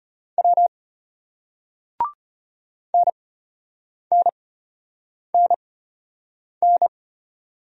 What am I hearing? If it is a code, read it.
Morse code, heard as a keyed tone, about 33 wpm, spaced as for 8 wpm, 700 Hz. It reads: W NNNN